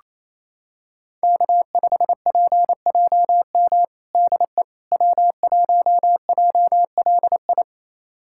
Morse code, heard as a keyed tone, about 28 wpm, 700 Hz. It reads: K5PJM DE W1JLI
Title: USA Callsign DE Callsign - 5 Characters - 28wpm